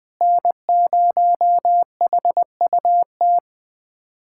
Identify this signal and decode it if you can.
Morse code, heard as a keyed tone, about 20 wpm, 700 Hz. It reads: N0HUT
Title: USA Callsign DE Callsign - 5 Characters - 20wpm